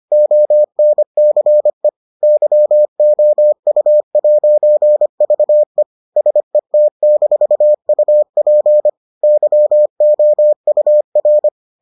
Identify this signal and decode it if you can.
Morse code, heard as a keyed tone, about 25 wpm, 600 Hz. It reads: ONCE YOU'VE SET-UP YOUR